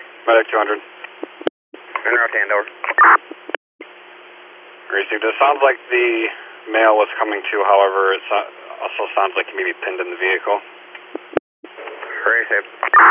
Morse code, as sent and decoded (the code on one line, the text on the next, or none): none